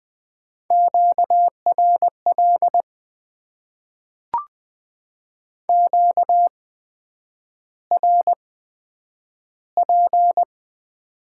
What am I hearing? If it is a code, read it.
Morse code, heard as a keyed tone, about 20 wpm, spaced as for 9 wpm, 700 Hz. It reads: QRL QRP
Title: Top 90 QSO Elements - 8x Character Spacing - 20wpm